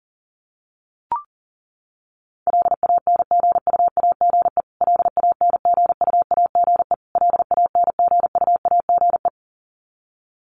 Morse code, heard as a keyed tone, about 40 wpm, 700 Hz. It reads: LANGUAGE LANGUAGE LANGUAGE
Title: Top 500 Words Repeated 40wpm